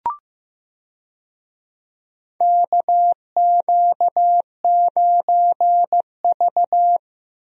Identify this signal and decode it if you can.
Morse code, heard as a keyed tone, about 15 wpm, 700 Hz. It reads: KQ9V
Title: USA Call Signs - 4 Characters - Repeated 3x - 15wpm